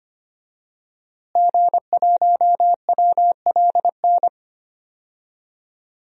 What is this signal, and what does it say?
Morse code, heard as a keyed tone, about 25 wpm, 700 Hz. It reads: G1WLN
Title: DXCC-100 Call Signs - 5 Characters - Repeated 3x - 25wpm